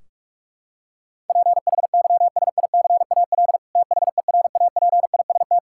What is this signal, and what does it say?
Morse code, heard as a keyed tone, about 41 wpm, 700 Hz. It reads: PHYSICAL THERAPIST
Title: Practice QSO - Exchange 3 - More Info - 45wpm